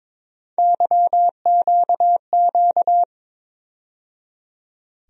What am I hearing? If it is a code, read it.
Morse code, heard as a keyed tone, about 21 wpm, 700 Hz. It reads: YQQ
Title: USA Call Sign Postfixes - 3 Letters 22wpm